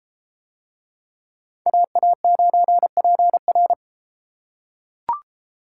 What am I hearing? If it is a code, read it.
Morse code, heard as a keyed tone, about 33 wpm, 700 Hz. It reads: AA9PR